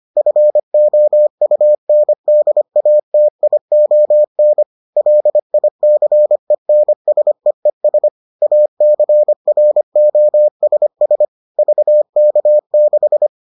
Morse code, heard as a keyed tone, about 25 wpm, 600 Hz. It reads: FOUNDATION LICENSEES ACROSS VK6